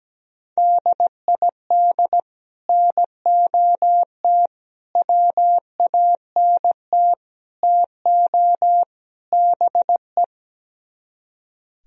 Morse code, heard as a keyed tone, about 17 wpm, 700 Hz. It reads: DID NOT WANT TO BE
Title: Speed-Racing - Sets of 5 Words 17wpm